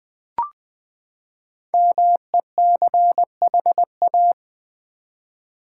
Morse code, heard as a keyed tone, about 20 wpm, 700 Hz. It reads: MECHA
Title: Build A Word - 5 Letters At A Time - 10-Letter Words - 20wpm